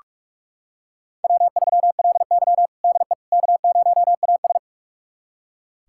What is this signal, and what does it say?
Morse code, heard as a keyed tone, about 45 wpm, 700 Hz. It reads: W3LY DE K0AS
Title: USA Callsign DE Callsign - 4 Characters - 45wpm